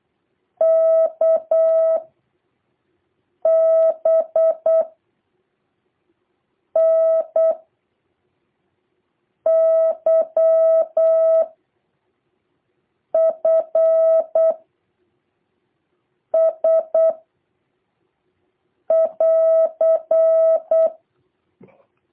Morse code, heard as a keyed tone, about 8 wpm, 650 Hz. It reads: K B N Y F S <AR>
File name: jh8cbh / 2023cw